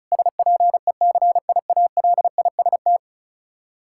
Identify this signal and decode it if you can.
Morse code, heard as a keyed tone, about 36 wpm, 700 Hz. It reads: SPECIALIST
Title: Practice QSO - Exchange 3 - More Info - 35wpm